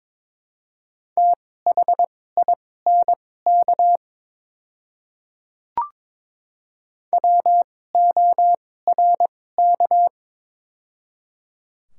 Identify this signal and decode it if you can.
Morse code, heard as a keyed tone, about 22 wpm, spaced as for 16 wpm, 700 Hz. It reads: THINK WORK
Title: 2x Character Spacing - Top 200 Words - 22wpm